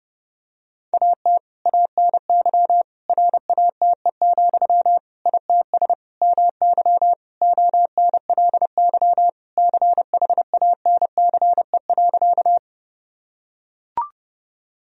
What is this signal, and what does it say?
Morse code, heard as a keyed tone, about 30 wpm, 700 Hz. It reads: AT ANY RATE, ITS MY ONLY CHANCE.
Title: Sentences from Top 1000 Words - Review Part 4 - 30wpm